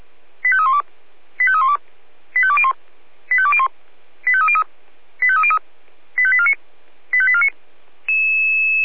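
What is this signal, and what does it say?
Five-tone selective call (ZVEI), heard as quick runs of five tones, tones of about 70 ms. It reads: 85312 85312 85201 85201 85303 85303 85849 85849
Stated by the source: ZVEI five-tone selective call received by SDR.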